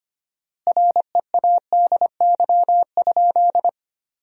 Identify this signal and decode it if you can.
Morse code, heard as a keyed tone, about 25 wpm, 700 Hz. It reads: READY?
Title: 4x Word-Spacing - Sentences from Top 400 Words - 25wpm